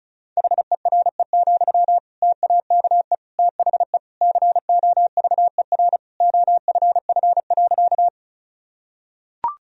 Morse code, heard as a keyed tone, about 36 wpm, 700 Hz. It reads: HERE, TAKE THE COVER OFF.
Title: Sentences from Top 1000 Words - Review Part 1 - 35wpm